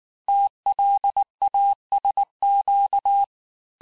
Morse code, heard as a keyed tone, about 19 wpm, 800 Hz. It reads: TLASQ
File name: cours-du-lundi files / cours-1583--texte-chiffre-18-mpm